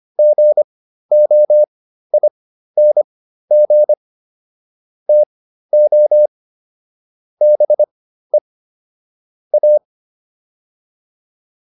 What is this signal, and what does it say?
Morse code, heard as a keyed tone, about 25 wpm, spaced as for 13 wpm, 600 Hz. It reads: GOING TO BE A